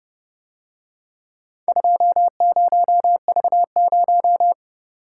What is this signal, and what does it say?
Morse code, heard as a keyed tone, about 29 wpm, 700 Hz. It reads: 20V0